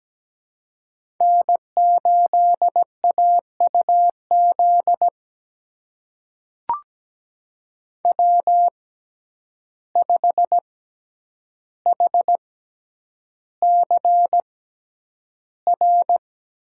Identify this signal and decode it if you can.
Morse code, heard as a keyed tone, about 17 wpm, spaced as for 9 wpm, 700 Hz. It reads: N8AUZ W5HCR
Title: USA Call Signs - 6x Character Spacing - 17wpm